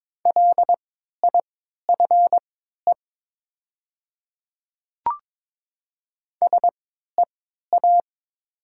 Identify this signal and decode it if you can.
Morse code, heard as a keyed tone, about 22 wpm, spaced as for 13 wpm, 690 Hz. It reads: LIFE SEA
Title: 3x Character Spacing - Top 500 Words - 22wpm